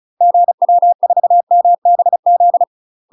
Morse code, heard as a keyed tone, about 35 wpm, 700 Hz. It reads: GW4MBZ
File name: UK MASTER QSOs TRAINING / 35WPM / UK MASTER QSOs 0015